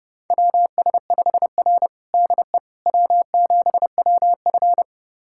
Morse code, heard as a keyed tone, about 31 wpm, 700 Hz. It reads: WS5R DE W7WF
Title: USA Callsign DE Callsign - 4 Characters - 30wpm